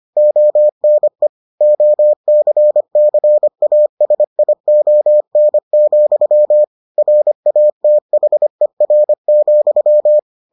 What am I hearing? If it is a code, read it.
Morse code, heard as a keyed tone, about 25 wpm, 600 Hz. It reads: ONE OCCASION, RATHER,